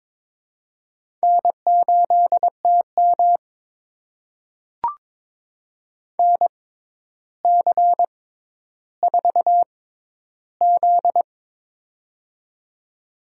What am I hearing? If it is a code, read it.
Morse code, heard as a keyed tone, about 22 wpm, spaced as for 11 wpm, 700 Hz. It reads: N8TM NC4Z
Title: USA Call Signs - 4 Characters - 6x Character Spacing - 22wpm